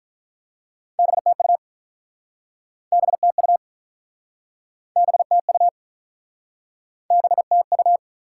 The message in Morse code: BTU BTU BTU BTU

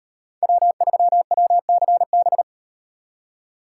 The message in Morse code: W3WCB

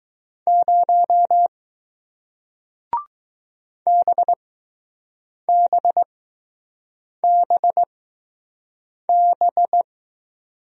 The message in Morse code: 0 BBBB

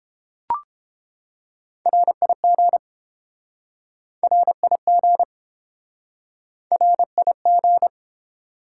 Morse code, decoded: RIG RIG RIG

